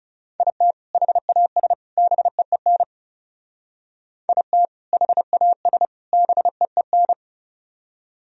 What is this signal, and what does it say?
Morse code, heard as a keyed tone, about 38 wpm, 700 Hz. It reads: IT HAS BEEN IT HAS BEEN